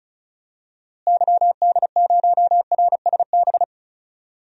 Morse code, heard as a keyed tone, about 36 wpm, 700 Hz. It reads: YD0RSB
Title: DXCC-100 Call Signs - 6 Characters - Repeated 3x - 35wpm